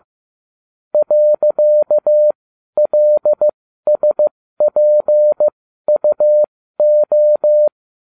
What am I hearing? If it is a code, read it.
Morse code, heard as a keyed tone, about 15 wpm, spaced as for 12 wpm, 600 Hz. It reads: .LSPUO